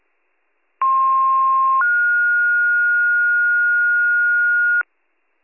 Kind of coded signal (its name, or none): two-tone page